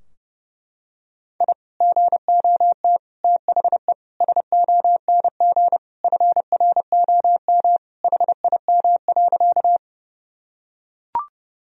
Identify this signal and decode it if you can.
Morse code, heard as a keyed tone, about 30 wpm, 700 Hz. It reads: I GOT THE SONG FROM HIM.
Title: Sentences from Top 1000 Words - Review Part 2 - 30wpm